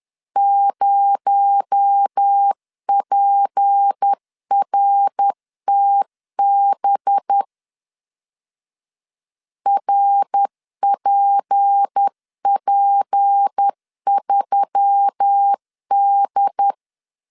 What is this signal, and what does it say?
Morse code, heard as a keyed tone, about 10 wpm, 800 Hz. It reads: 0PRTB RPP3D
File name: CW / DET / 1 6 1500C S C 10wpm 4252